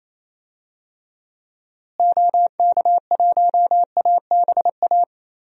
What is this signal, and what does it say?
Morse code, heard as a keyed tone, about 28 wpm, 700 Hz. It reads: OK1ABA
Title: DXCC-100 Call Signs - 6 Characters - Repeated 3x - 28wpm